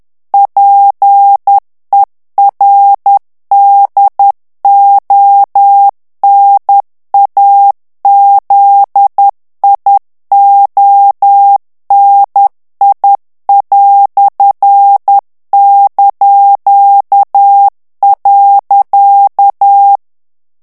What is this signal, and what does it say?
Morse code, heard as a keyed tone, about 10 wpm, 800 Hz. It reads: PER DONAZIONI").
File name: morse mp3 / ITALIEN texte clair 1 vitesse 10 mots minute